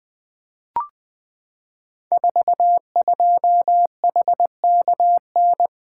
Morse code, heard as a keyed tone, about 20 wpm, 700 Hz. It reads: 42HKN
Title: Code Groups - 5 Letter:Numbers - 20wpm